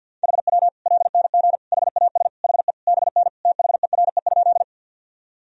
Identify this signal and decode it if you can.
Morse code, heard as a keyed tone, about 47 wpm, 700 Hz. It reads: HW LNG HAS HE BN THERE?
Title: Sentences from Top 100 Words and Abbreviations 50wpm